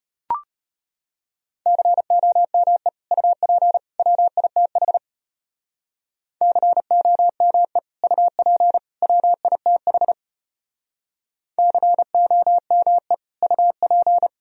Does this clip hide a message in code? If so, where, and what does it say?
Yes: Morse code from 1.7 to 14.3 s, heard as a keyed tone, about 33 wpm, 700 Hz. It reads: COME UP WITH COME UP WITH COME UP